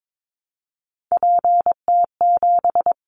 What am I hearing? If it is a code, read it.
Morse code, heard as a keyed tone, about 22 wpm, 700 Hz. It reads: PT7
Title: Callsign Prefixes for Top 50 DXCC - 22wpm